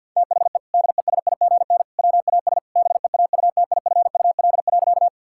Morse code, heard as a keyed tone, about 50 wpm, 700 Hz. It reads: THE DESIGN WAS BEAUTIFUL.
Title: Sentences from Top 1000 Words - Review Part 5 - 50wpm